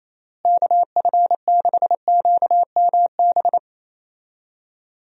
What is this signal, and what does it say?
Morse code, heard as a keyed tone, about 27 wpm, 700 Hz. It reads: KF6QMB
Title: USA Call Signs - 6 Characters - Repeated 3x - 28wpm